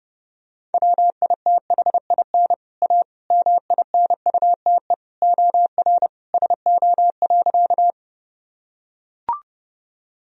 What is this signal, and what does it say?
Morse code, heard as a keyed tone, about 30 wpm, 700 Hz. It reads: WITHIN A MINUTE OR SO.
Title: Sentences from Top 1000 Words - Review Part 4 - 30wpm